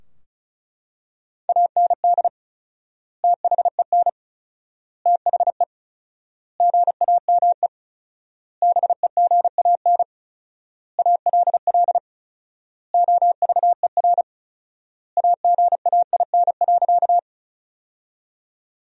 Morse code, heard as a keyed tone, about 34 wpm, 700 Hz. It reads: AND THEN THE GAME BEGAN ALL OVER AGAIN.
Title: 4x Word-Spacing - Sentences from Top 500 Words - 35wpm